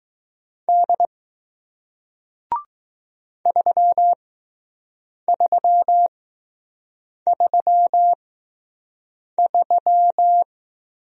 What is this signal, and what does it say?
Morse code, heard as a keyed tone, about 20 wpm, spaced as for 6 wpm, 700 Hz. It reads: D 3333